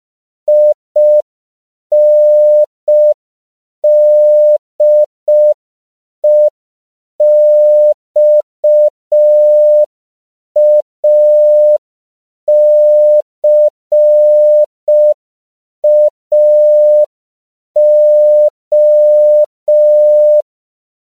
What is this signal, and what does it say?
Morse code, heard as a keyed tone, about 5 wpm, 600 Hz. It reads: INDEXACAO
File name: CW / 5PPM